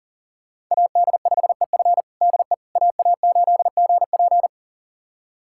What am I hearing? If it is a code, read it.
Morse code, heard as a keyed tone, about 40 wpm, 700 Hz. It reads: AD5EF DE AA8GP
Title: USA Callsign DE Callsign - 5 Characters - 40wpm